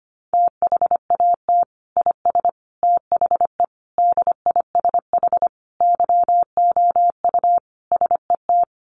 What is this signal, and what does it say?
Morse code, heard as a keyed tone, about 25 wpm, 700 Hz. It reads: THAT IS THE DISH YOU SET
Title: Sentences from Top 1300 Words 25wpm